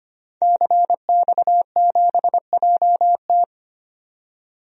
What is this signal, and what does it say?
Morse code, heard as a keyed tone, about 25 wpm, 690 Hz. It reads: CX7JT
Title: DXCC-100 Call Signs - 5 Characters - 25wpm